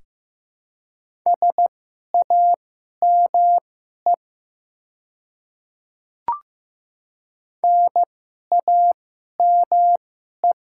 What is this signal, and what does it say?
Morse code, heard as a keyed tone, about 15 wpm, spaced as for 11 wpm, 700 Hz. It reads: SAME NAME
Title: 2x Character Spacing - Top 200 Words - 15wpm